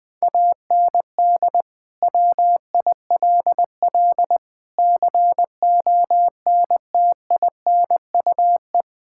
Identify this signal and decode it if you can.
Morse code, heard as a keyed tone, about 20 wpm, 700 Hz. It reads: AND WILL CONTINUE
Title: Sets of 6 Words - 20wpm